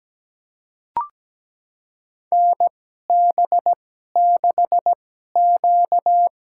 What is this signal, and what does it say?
Morse code, heard as a keyed tone, about 17 wpm, spaced as for 12 wpm, 700 Hz. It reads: NB6Q
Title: USA Call Signs - 4 Characters - 2x Character Spacing - 17wpm